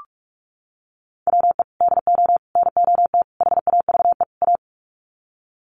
Morse code, heard as a keyed tone, about 44 wpm, 700 Hz. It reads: WE DO NOT HAVE A
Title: Sets of 5 Words 45wpm